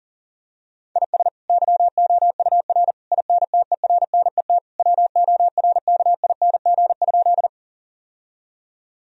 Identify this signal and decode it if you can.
Morse code, heard as a keyed tone, about 40 wpm, 700 Hz. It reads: IS YOUR INTERNET WORKING?
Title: Sentences from Top 1000 Words - Review Part 4 - 40wpm